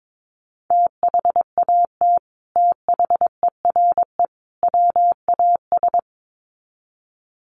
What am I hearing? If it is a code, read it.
Morse code, heard as a keyed tone, about 22 wpm, 700 Hz. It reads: THAT THERE WAS
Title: Sets of 3 Words 22wpm